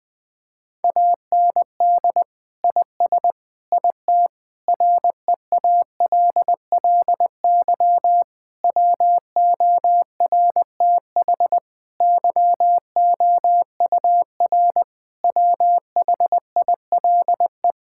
Morse code, heard as a keyed tone, about 20 wpm, 700 Hz. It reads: AND IS IT REALLY WORTH YOUR WHILE
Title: Sentences from Top 1600 Words 20wpm